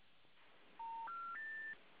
Special information tone, with noise.